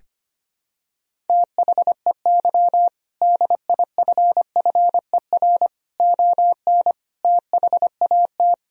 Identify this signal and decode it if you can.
Morse code, heard as a keyed tone, about 25 wpm, 700 Hz. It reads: THEY DIFFER ON THAT
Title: Sentences from Top 1000 Words - Review Part 1 - 25wpm